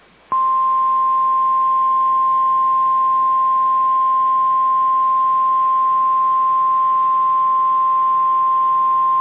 Emergency Alert System (SAME) tones (the 1050 Hz attention tone). It is heard from 1.0 to 8.5 s.